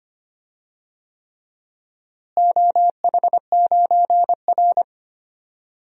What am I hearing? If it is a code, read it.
Morse code, heard as a keyed tone, about 25 wpm, 700 Hz. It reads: OH9R
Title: DXCC-100 Call Signs - 4 Characters - Repeated 3x - 25wpm